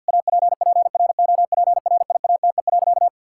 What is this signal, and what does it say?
Morse code, heard as a keyed tone, about 50 wpm, 700 Hz. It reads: APPROPRIATE.
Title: Sentences from Top 1000 Words - Review Part 5 - 50wpm